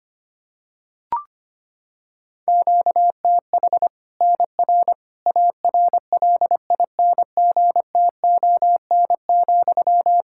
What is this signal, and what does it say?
Morse code, heard as a keyed tone, about 25 wpm, 700 Hz. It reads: QTH NR ARLINGTON,
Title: K1USN SST QSO Practice 25wpm